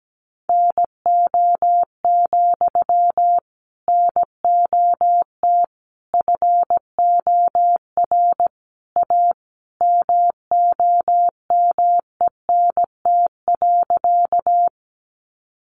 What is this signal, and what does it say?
Morse code, heard as a keyed tone, about 17 wpm, 700 Hz. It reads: NO, NOT FOR A MOMENT.
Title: Sentences from Top 1000 Words 17wpm